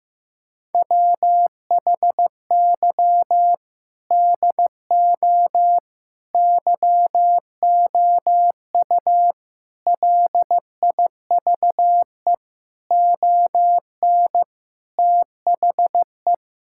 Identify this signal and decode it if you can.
Morse code, heard as a keyed tone, about 15 wpm, 700 Hz. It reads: WHY DO YOU LIVE ON THE
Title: Sentences from Top 1000 Words - Review Part 6 - 15wpm